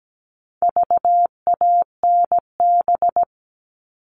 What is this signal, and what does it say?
Morse code, heard as a keyed tone, about 17 wpm, 700 Hz. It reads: VANB
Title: All Cut Numbers - 4-Digit 17wpm